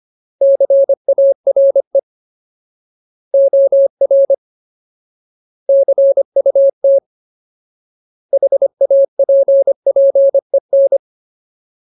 Morse code, heard as a keyed tone, about 25 wpm, 550 Hz. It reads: CARE OR CUT HAPPEN
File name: Ultimate Word List / 25WPM Ultimate